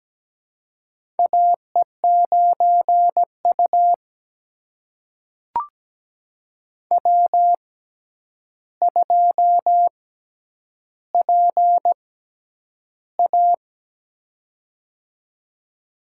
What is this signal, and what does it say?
Morse code, heard as a keyed tone, about 17 wpm, spaced as for 9 wpm, 700 Hz. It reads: AE9U W2PA
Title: USA Call Signs - 4 Characters - 6x Character Spacing - 17wpm